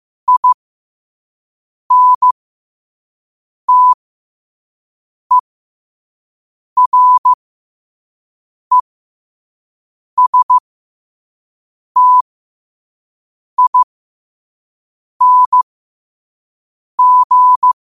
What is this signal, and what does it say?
Morse code, heard as a keyed tone, about 15 wpm, spaced as for 5 wpm, 1000 Hz. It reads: INTERESTING
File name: 5wpm